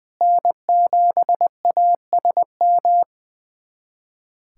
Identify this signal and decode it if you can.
Morse code, heard as a keyed tone, about 20 wpm, 700 Hz. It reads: N7ASM
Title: USA Callsign DE Callsign - 5 Characters - 20wpm